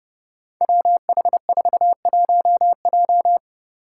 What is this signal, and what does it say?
Morse code, heard as a keyed tone, about 30 wpm, 700 Hz. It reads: WH41J
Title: Five Challenging Letter-Number Combinations 30wpm